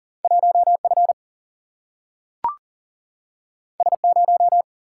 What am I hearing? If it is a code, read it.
Morse code, heard as a keyed tone, about 40 wpm, 700 Hz. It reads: 1F S0
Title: Single-Letter Single-Number 40wpm